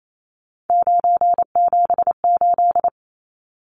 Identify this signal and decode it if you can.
Morse code, heard as a keyed tone, about 28 wpm, 700 Hz. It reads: 978
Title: Numbers - 3-Digit 28wpm